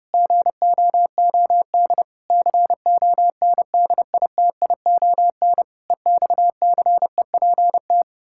Morse code, heard as a keyed tone, about 30 wpm, 700 Hz. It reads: GOOD CONDITION EXCEPT
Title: Pangram Challenge v2 - 30wpm